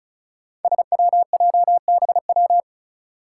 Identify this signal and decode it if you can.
Morse code, heard as a keyed tone, about 34 wpm, 700 Hz. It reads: SWJBW
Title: Five Challenging Letter-Number Combinations 35wpm